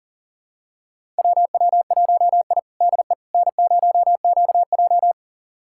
Morse code, heard as a keyed tone, about 40 wpm, 700 Hz. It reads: WW1I DE N0QJ
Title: USA Callsign DE Callsign - 4 Characters - 40wpm